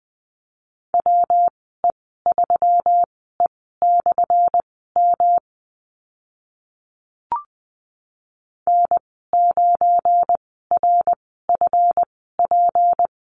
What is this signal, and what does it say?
Morse code, heard as a keyed tone, about 20 wpm, spaced as for 15 wpm, 700 Hz. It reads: WE3E/M N9RFP
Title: 179 Lesson 29g - Slash - Call Signs 20wpm